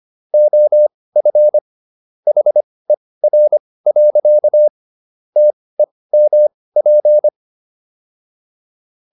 Morse code, heard as a keyed tone, about 25 wpm, spaced as for 18 wpm, 600 Hz. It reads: OF HER. TEMP